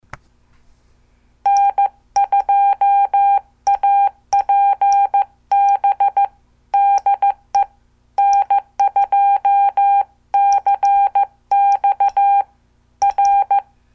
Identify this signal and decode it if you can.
Morse code, heard as a keyed tone, about 14 wpm, 800 Hz. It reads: N2APB DE N2CX R
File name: morse / qso